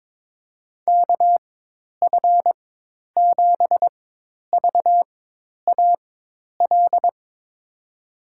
Morse code, heard as a keyed tone, about 22 wpm, spaced as for 10 wpm, 700 Hz. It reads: KF7VAL
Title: USA Call Signs - 6 Characters - 4x Character Spacing - 22wpm